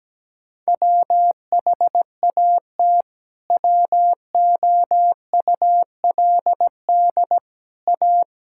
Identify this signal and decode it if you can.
Morse code, heard as a keyed tone, about 17 wpm, 700 Hz. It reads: WHAT WOULD A
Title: Sentences from Top 1000 Words - Review Part 6 - 17wpm